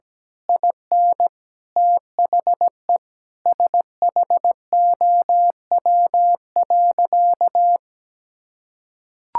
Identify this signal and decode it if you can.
Morse code, heard as a keyed tone, about 17 wpm, 700 Hz. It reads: IN THE SHOW.
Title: Sentences from Top 1000 Words - Review Part 6 - 17wpm